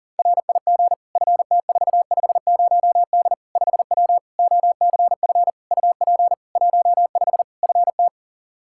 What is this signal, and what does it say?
Morse code, heard as a keyed tone, about 40 wpm, 700 Hz. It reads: RIG FT450D 5W OCF UP 15 FT
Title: K1USN SST QSO Practice 40wpm